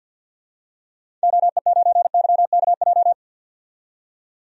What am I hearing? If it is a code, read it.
Morse code, heard as a keyed tone, about 51 wpm, 700 Hz. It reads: OE9YKJ
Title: DXCC-100 Call Signs - 6 Characters - 50wpm